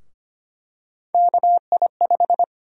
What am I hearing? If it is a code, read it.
Morse code, heard as a keyed tone, about 25 wpm, 700 Hz. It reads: KI5
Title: USA Call Sign Prefixes - Repeated 2x - 25wpm